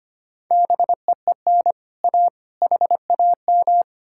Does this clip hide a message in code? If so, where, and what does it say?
Yes: Morse code from 0.5 to 3.8 s, heard as a keyed tone, about 33 wpm, 700 Hz. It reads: BEEN A HAM